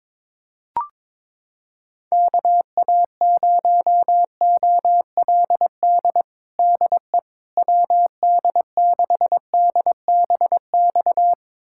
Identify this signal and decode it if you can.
Morse code, heard as a keyed tone, about 22 wpm, 700 Hz. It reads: KA0OLD DE WD6DBX
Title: USA Callsign DE Callsign - 6 Characters - 22wpm